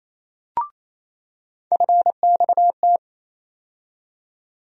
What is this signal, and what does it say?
Morse code, heard as a keyed tone, about 28 wpm, 700 Hz. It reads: FXT
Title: USA Call Sign Postfixes - 3 Letters 28wpm